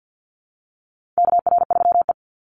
Morse code, heard as a keyed tone, about 50 wpm, 700 Hz. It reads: KR3E
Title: Call Signs - 50wpm